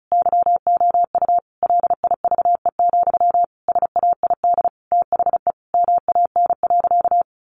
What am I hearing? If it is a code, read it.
Morse code, heard as a keyed tone, about 34 wpm, 700 Hz. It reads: YOU LIVE, SAID THE MAN.
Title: Sentences from Top 200 Words 35wpm